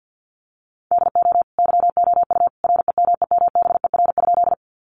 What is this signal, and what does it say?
Morse code, heard as a keyed tone, about 50 wpm, 700 Hz. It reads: DO YOU REMEMBER?
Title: Sentences from Top 400 Words 50wpm